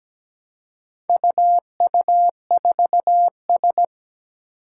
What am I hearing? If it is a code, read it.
Morse code, heard as a keyed tone, about 17 wpm, 700 Hz. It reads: UU4S
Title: Four Challenging Letter-Number Combinations 17wpm